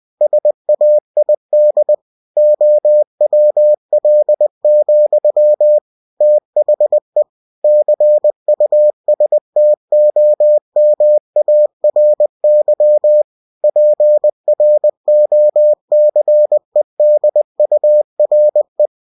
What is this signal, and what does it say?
Morse code, heard as a keyed tone, about 20 wpm, 600 Hz. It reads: SAID OWL, THE CUSTOMARY PROCEDURE
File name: cw-books / winnie-the-pooh / winnie-20wpm-page-0029